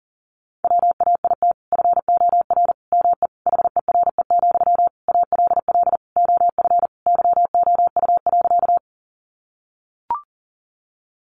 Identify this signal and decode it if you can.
Morse code, heard as a keyed tone, about 40 wpm, 700 Hz. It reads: WAIT FOR ME HERE, ALL OF YOU.
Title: Sentences from Top 500 Words 40wpm